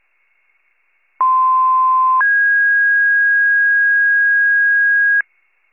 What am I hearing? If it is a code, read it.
Two-tone page, heard as a one-second tone, then a three-second tone, about 1040 then 1650 Hz. It is heard from 1.2 to 5.2 s.
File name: pager / 95 RESCUE DUTY / wcalert